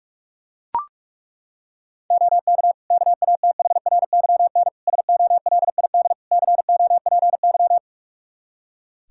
Morse code, heard as a keyed tone, about 41 wpm, 700 Hz. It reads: OK KATHRYN SOLID COPY
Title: K1USN SST QSO Practice 45wpm